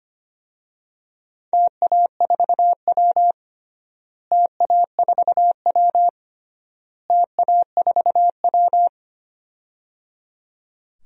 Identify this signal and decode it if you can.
Morse code, heard as a keyed tone, about 25 wpm, 700 Hz. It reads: TA4W TA4W TA4W
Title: DXCC-100 Call Signs - 4 Characters - Repeated 3x - 25wpm